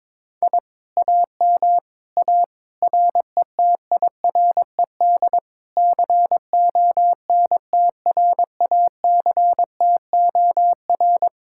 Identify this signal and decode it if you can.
Morse code, heard as a keyed tone, about 22 wpm, 700 Hz. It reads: I AM A RETIRED CONTRACTOR
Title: Speed-Racing Common QSO Phrases - 22wpm